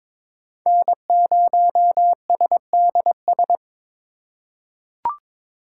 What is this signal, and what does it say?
Morse code, heard as a keyed tone, about 21 wpm, 700 Hz. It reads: N0SDS
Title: USA Call Signs - 5 Characters - 8x Character Spacing - 22wpm